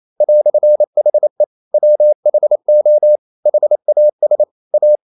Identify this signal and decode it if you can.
Morse code, heard as a keyed tone, about 28 wpm, 600 Hz. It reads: "HE WHO HAS A
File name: CW / Practice / EnglishQuotations0228